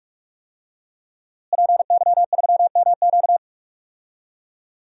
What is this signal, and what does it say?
Morse code, heard as a keyed tone, about 46 wpm, 700 Hz. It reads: PY3MQ
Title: DXCC-100 Call Signs - 5 Characters - Repeated 2x - 45wpm